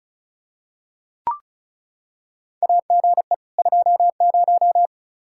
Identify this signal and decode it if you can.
Morse code, heard as a keyed tone, about 36 wpm, 700 Hz. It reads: AGE 20
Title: Practice QSO - Exchange 3 - More Info - 35wpm